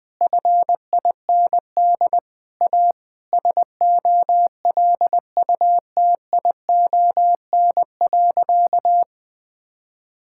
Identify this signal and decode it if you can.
Morse code, heard as a keyed tone, about 20 wpm, 700 Hz. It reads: FIND A SOLUTION.
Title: Sentences with 8-Word Phrases - Spoken Beginning - 20wpm